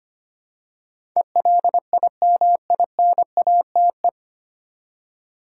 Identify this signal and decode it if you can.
Morse code, heard as a keyed tone, about 25 wpm, 700 Hz. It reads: ELIMINATE
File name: practice / 25wpm / 9 Letter Words - First 3 Letters Given - 25wpm